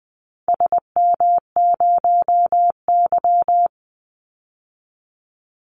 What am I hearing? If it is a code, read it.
Morse code, heard as a keyed tone, about 20 wpm, 700 Hz. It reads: SM0Y
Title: Call Signs - 20wpm